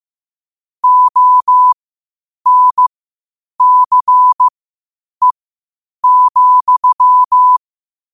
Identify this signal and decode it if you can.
Morse code, heard as a keyed tone, about 15 wpm, spaced as for 8 wpm, 1000 Hz. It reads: ONCE,
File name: qotdfiles / 8wpm